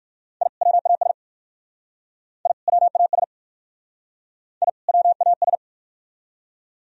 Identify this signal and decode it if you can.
Morse code, heard as a keyed tone, about 51 wpm, 700 Hz. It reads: I WAS I WAS I WAS
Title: Speed-Racing - Sets of 2 Words 45wpm